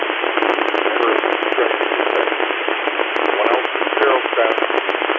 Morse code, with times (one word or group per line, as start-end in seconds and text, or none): none